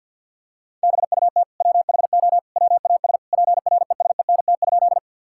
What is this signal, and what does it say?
Morse code, heard as a keyed tone, about 50 wpm, 700 Hz. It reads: BUT WHO WAS PRESENT?
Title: Sentences from Top 1000 Words - Review Part 5 - 50wpm